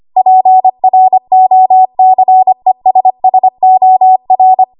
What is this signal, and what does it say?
Morse code, heard as a keyed tone, about 25 wpm, 750 Hz. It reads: PROCESSOR
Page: W1AW Web Code Practice Files